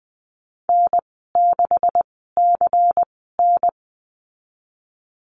Morse code, heard as a keyed tone, about 20 wpm, spaced as for 14 wpm, 700 Hz. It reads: N6CN
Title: 167 Lesson 27g - Number 7 - Call Signs 20wpm